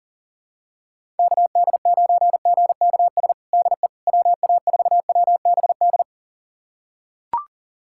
Morse code, heard as a keyed tone, about 40 wpm, 700 Hz. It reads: KD9GKS DE WA4WBD
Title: USA Callsign DE Callsign - 6 Characters - 40wpm